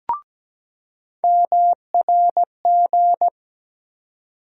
Morse code, heard as a keyed tone, about 17 wpm, 700 Hz. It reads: MRG
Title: USA Call Sign Postfixes - 3 Letters 17wpm